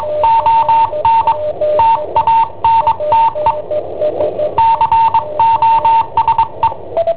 Morse code, heard as a keyed tone, about 22 wpm, 910 Hz. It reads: ON TANTE COSE
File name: i7ohp / cassetta i7ohp B